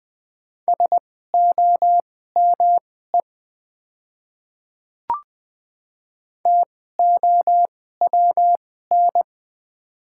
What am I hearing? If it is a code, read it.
Morse code, heard as a keyed tone, about 20 wpm, spaced as for 14 wpm, 700 Hz. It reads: SOME TOWN
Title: 2x Character Spacing - Top 500 Words - 20wpm